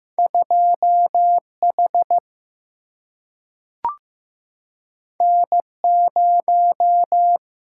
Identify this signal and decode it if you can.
Morse code, heard as a keyed tone, about 15 wpm, 700 Hz. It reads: 2H N0
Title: Single-Letter Single-Number 15wpm